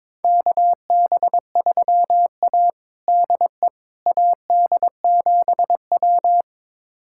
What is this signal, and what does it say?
Morse code, heard as a keyed tone, about 22 wpm, 700 Hz. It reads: KB3A DE AD7W
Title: USA Callsign DE Callsign - 4 Characters - 22wpm